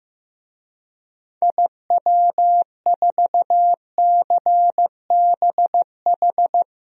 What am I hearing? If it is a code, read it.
Morse code, heard as a keyed tone, about 15 wpm, 700 Hz. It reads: IW4CBH
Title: DXCC-100 Call Signs - 6 Characters - 15wpm